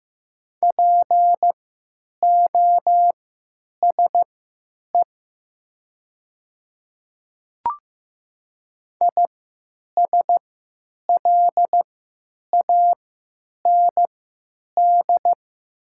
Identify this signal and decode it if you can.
Morse code, heard as a keyed tone, about 15 wpm, spaced as for 9 wpm, 700 Hz. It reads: POSE ISLAND
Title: 3x Character Spacing - Top 500 Words - 15wpm